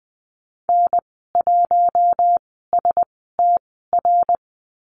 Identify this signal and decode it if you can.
Morse code, heard as a keyed tone, about 20 wpm, spaced as for 14 wpm, 700 Hz. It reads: N1STR
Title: 082 Lesson 14g - Number 2 - Call Signs 20wpm